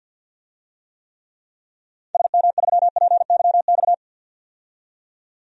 Morse code, heard as a keyed tone, about 47 wpm, 700 Hz. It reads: SM3PYX
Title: DXCC-100 Call Signs - 6 Characters - 50wpm